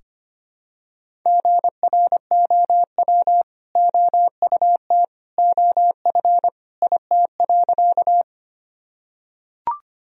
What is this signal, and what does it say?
Morse code, heard as a keyed tone, about 25 wpm, 700 Hz. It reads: GROW OUT OF IT.
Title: Sentences from Top 1000 Words - Review Part 1 - 25wpm